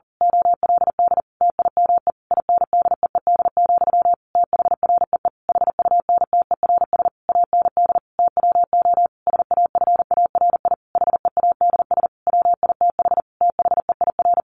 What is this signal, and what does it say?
Morse code, heard as a keyed tone, about 40 wpm, 700 Hz. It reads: OLD TIME INDEED, THREE HUNTERS AND TWO SAFARI HEADS WITH THEIR
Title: A Sound Of Thunder 40wpm